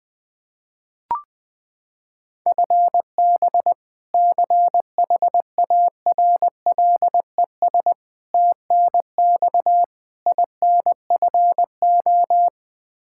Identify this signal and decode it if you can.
Morse code, heard as a keyed tone, about 20 wpm, 700 Hz. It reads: FB CHARLES TNX INFO
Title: Practice QSO - Exchange 3 - More Info - 20wpm